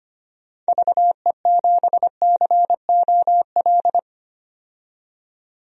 Morse code, heard as a keyed tone, about 25 wpm, 700 Hz. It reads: VE7COL
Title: DXCC-100 Call Signs - 6 Characters - 25wpm